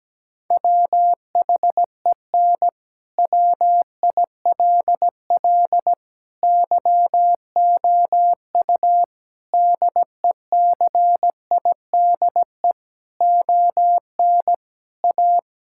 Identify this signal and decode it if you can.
Morse code, heard as a keyed tone, about 17 wpm, 700 Hz. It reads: WHEN WILL YOU DECIDE ON A
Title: Sentences from Top 1000 Words - Review Part 6 - 17wpm